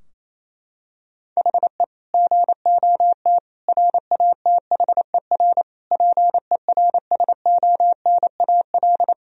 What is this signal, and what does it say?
Morse code, heard as a keyed tone, about 28 wpm, 700 Hz. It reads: HE GOT RATHER PERSONAL
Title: Sentences from Top 1000 Words - Review Part 4 - 28wpm